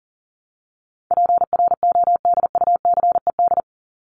Morse code, heard as a keyed tone, about 40 wpm, 700 Hz. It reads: PRODUCED
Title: ED Suffix - Top 100 Words with Base Words 40wpm